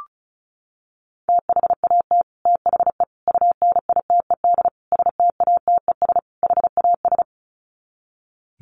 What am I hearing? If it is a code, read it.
Morse code, heard as a keyed tone, about 34 wpm, 700 Hz. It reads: THAT THE UNITED STATES HAS
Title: Sets of 5 Words Encore 35wpm